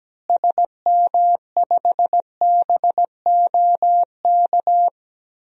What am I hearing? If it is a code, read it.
Morse code, heard as a keyed tone, about 17 wpm, 700 Hz. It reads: SM5BOK
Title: DXCC-100 Call Signs - 6 Characters - Repeated 3x - 17wpm